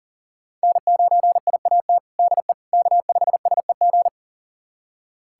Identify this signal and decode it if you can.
Morse code, heard as a keyed tone, about 40 wpm, 700 Hz. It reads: N9IAT DE K5SEG
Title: USA Callsign DE Callsign - 5 Characters - 40wpm